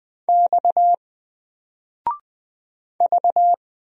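Morse code, heard as a keyed tone, about 20 wpm, 700 Hz. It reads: X V
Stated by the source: Single letter and number Morse drill at 20 wpm, from N4LCD.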